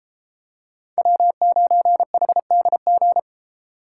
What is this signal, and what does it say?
Morse code, heard as a keyed tone, about 31 wpm, 700 Hz. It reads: W9HDG